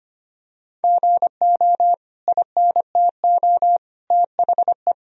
Morse code, heard as a keyed tone, about 25 wpm, 700 Hz. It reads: GO INTO THE
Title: Sentences from Top 1000 Words - Review Part 4 - 25wpm